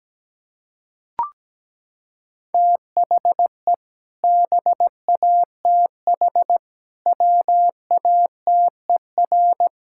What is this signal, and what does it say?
Morse code, heard as a keyed tone, about 17 wpm, 700 Hz. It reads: THE BATH WATER
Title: Sentences from Top 1600 Words 17wpm